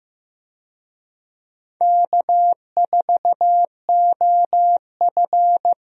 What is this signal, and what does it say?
Morse code, heard as a keyed tone, about 15 wpm, 700 Hz. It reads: K4OF
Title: USA Call Signs - 4 Characters - Repeated 2x - 15wpm